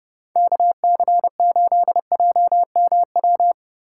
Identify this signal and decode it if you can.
Morse code, heard as a keyed tone, about 39 wpm, 700 Hz. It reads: KC8JMW